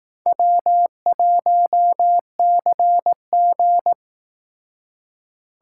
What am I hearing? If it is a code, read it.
Morse code, heard as a keyed tone, about 17 wpm, 700 Hz. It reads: W1CG
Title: Speed Racing - USA Call Signs 15wpm